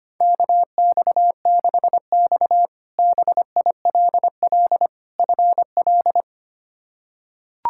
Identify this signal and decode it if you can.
Morse code, heard as a keyed tone, about 25 wpm, 700 Hz. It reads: KX6X BILL FL
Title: K1USN SST QSO Practice - 25wpm